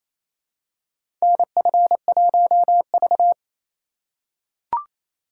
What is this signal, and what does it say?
Morse code, heard as a keyed tone, about 28 wpm, 700 Hz. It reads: NF1V
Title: USA Call Signs - 4 Characters - 3x Character Spacing - 28wpm